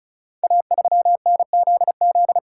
Morse code, heard as a keyed tone, about 34 wpm, 700 Hz. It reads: A3NZZ
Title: Code Groups - 5 Letter:Numbers - 35wpm